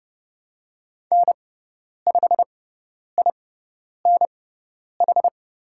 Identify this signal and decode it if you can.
Morse code, heard as a keyed tone, about 30 wpm, spaced as for 10 wpm, 700 Hz. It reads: N5INH
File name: Lesson 16g - Letter c - Call Signs 30wpm